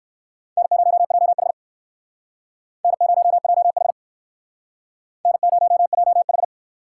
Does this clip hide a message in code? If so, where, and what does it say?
Yes: Morse code from 0.6 to 6.4 s, heard as a keyed tone, about 59 wpm, 700 Hz. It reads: N0JH N0JH N0JH